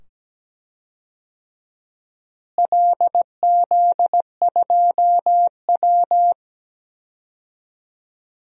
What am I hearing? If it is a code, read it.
Morse code, heard as a keyed tone, about 17 wpm, 700 Hz. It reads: LZ2W
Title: DXCC-100 Call Signs - 4 Characters - Repeated 2x - 17wpm